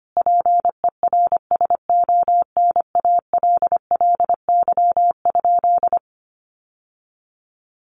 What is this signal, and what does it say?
Morse code, heard as a keyed tone, about 25 wpm, 700 Hz. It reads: PERSONALLY?
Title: Sentences from Top 1400 Words 25wpm